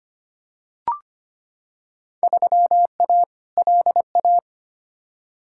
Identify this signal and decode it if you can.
Morse code, heard as a keyed tone, about 25 wpm, 700 Hz. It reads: 3A LA
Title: Field Day Exchanges - 25wpm